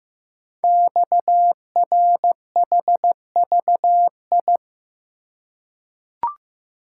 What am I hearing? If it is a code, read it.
Morse code, heard as a keyed tone, about 15 wpm, 700 Hz. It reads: XRHVI
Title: Code Groups - 5 Letters - 15wpm